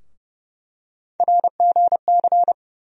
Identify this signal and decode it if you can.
Morse code, heard as a keyed tone, about 32 wpm, 700 Hz. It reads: RGC
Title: USA Call Sign Postfixes - 3 Letters 30wpm